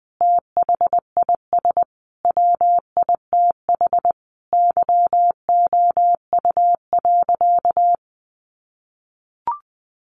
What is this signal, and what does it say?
Morse code, heard as a keyed tone, about 19 wpm, 700 Hz. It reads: THIS WITH YOU.
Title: Sentences from Top 500 Words 20wpm